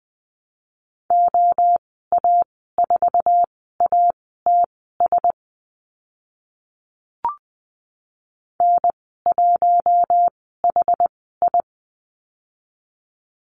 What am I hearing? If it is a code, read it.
Morse code, heard as a keyed tone, about 20 wpm, spaced as for 14 wpm, 700 Hz. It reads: OA4ATS N1HI
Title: 062 Lesson 11g - Letter h - Call Signs 20wpm